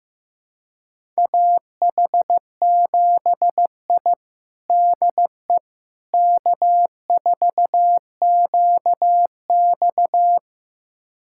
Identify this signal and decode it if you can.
Morse code, heard as a keyed tone, about 16 wpm, 700 Hz. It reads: AH7I DE K4QX
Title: USA Callsign DE Callsign - 4 Characters - 15wpm